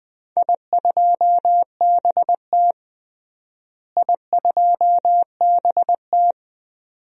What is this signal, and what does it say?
Morse code, heard as a keyed tone, about 20 wpm, 700 Hz. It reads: I2BT I2BT
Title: DXCC-100 Call Signs - 4 Characters - Repeated 2x - 20wpm